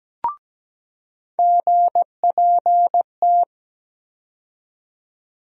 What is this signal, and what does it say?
Morse code, heard as a keyed tone, about 17 wpm, 700 Hz. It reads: GPT
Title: USA Call Sign Postfixes - 3 Letters 17wpm